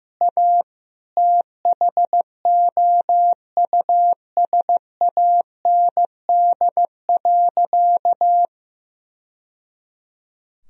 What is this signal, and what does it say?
Morse code, heard as a keyed tone, about 15 wpm, 700 Hz. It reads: A THOUSAND.
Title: Sentences from Top 1000 Words - Review Part 3 - 15wpm